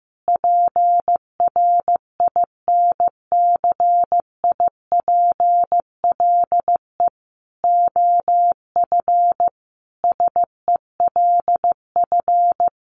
Morse code, heard as a keyed tone, about 15 wpm, 700 Hz. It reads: PRINCIPLE OF SELF
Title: Sentences from Top 1400 Words 15wpm